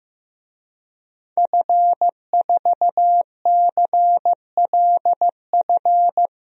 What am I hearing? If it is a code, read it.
Morse code, heard as a keyed tone, about 15 wpm, 700 Hz. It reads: F4CLF
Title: DXCC-100 Call Signs - 5 Characters - 15wpm